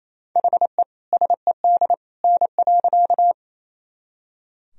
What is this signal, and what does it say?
Morse code, heard as a keyed tone, about 27 wpm, 700 Hz. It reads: HE SED N.
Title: Sentences from Top 100 Words and Abbreviations 28wpm